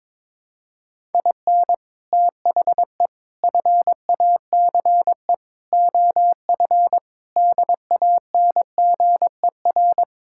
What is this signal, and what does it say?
Morse code, heard as a keyed tone, about 22 wpm, 700 Hz. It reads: IN THE FACE OF DANGER
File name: practice / 22wpm / Sentences with 3-Word Phrases - 22wpm